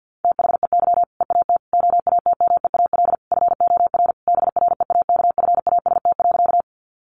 Morse code, heard as a keyed tone, about 50 wpm, 700 Hz. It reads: THEY EAT OATMEAL FOR BREAKFAST.
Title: Easy Phrases 50wpm